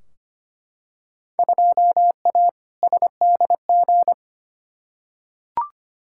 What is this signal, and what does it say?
Morse code, heard as a keyed tone, about 25 wpm, 700 Hz. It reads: 2A SDG
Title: Field Day Exchanges Encore - 25wpm